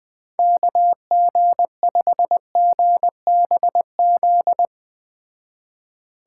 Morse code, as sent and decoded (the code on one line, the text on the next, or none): -.- --. ..... --. -... --..
KG5GBZ